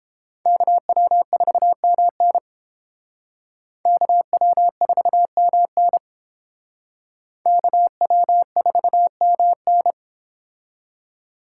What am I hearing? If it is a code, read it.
Morse code, heard as a keyed tone, about 30 wpm, 700 Hz. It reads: KW4MN KW4MN KW4MN